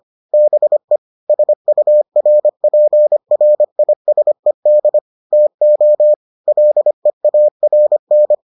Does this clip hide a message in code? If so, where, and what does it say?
Yes: Morse code from 0.3 to 8.3 s, heard as a keyed tone, about 25 wpm, 600 Hz. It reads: BE SURPRISED TO LEARN